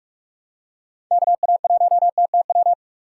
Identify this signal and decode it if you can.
Morse code, heard as a keyed tone, about 46 wpm, 700 Hz. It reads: KA1TTW
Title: USA Call Signs - 6 Characters - 4x Character Spacing - 45wpm